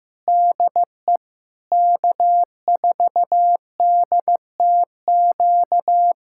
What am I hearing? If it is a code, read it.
Morse code, heard as a keyed tone, about 15 wpm, 700 Hz. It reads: DE K4DTQ
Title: USA Callsign DE Callsign 15wpm